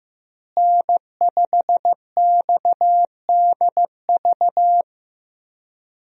Morse code, heard as a keyed tone, about 15 wpm, 700 Hz. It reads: N5XDV
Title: USA Call Signs - 5 Characters - Repeated 2x - 15wpm